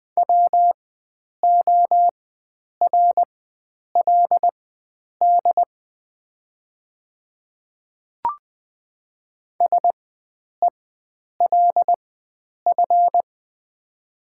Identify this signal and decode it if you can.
Morse code, heard as a keyed tone, about 20 wpm, spaced as for 9 wpm, 700 Hz. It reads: WORLD SELF
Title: 4x Character Spacing - Top 300 Words - 20wpm